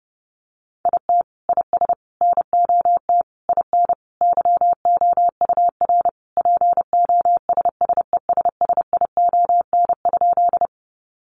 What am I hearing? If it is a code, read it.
Morse code, heard as a keyed tone, about 30 wpm, 700 Hz. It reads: IT IS NOT IN YOUR POSSESSION?
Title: Sentences from Top 1400 Words 30wpm